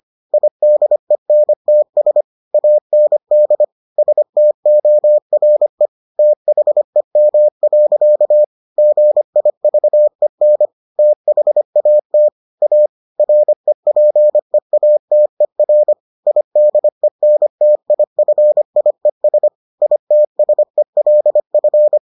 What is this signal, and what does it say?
Morse code, heard as a keyed tone, about 25 wpm, 600 Hz. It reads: IDENTS AND STORE THEM. GIVEN THAT A REPEATER IDENTIFIES ITSELF